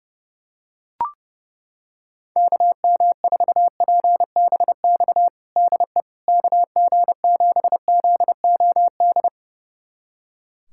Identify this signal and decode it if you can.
Morse code, heard as a keyed tone, about 30 wpm, 700 Hz. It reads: KM4PBX DE KG7ZOD
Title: USA Callsign DE Callsign - 6 Characters - 30wpm